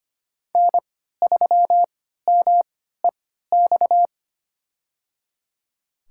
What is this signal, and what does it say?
Morse code, heard as a keyed tone, about 25 wpm, spaced as for 14 wpm, 700 Hz. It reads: N3MEX
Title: USA Call Signs - 5 Characters - 3x Character Spacing - 25wpm